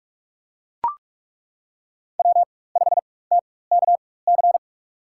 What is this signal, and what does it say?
Morse code, heard as a keyed tone, about 42 wpm, spaced as for 20 wpm, 700 Hz. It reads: W5TKC
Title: USA Call Signs - 5 Characters - 4x Character Spacing - 45wpm